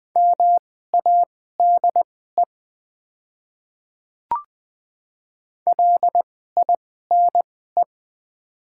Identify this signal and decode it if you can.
Morse code, heard as a keyed tone, about 20 wpm, spaced as for 14 wpm, 700 Hz. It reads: MADE LINE
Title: 2x Character Spacing - Top 200 Words - 20wpm